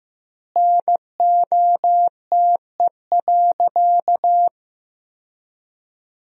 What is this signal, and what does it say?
Morse code, heard as a keyed tone, about 15 wpm, 700 Hz. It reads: NOTE.
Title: 2x Word-Spacing - Sentences from Top 500 Words - 15wpm